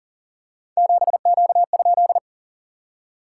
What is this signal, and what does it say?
Morse code, heard as a keyed tone, about 40 wpm, 690 Hz. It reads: 7Q?
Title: SOTA QSO Practice 40wpm